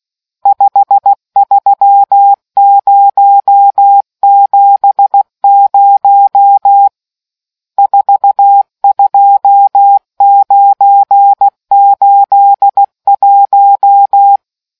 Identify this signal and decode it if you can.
Morse code, heard as a keyed tone, about 16 wpm, 800 Hz. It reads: 53070 42981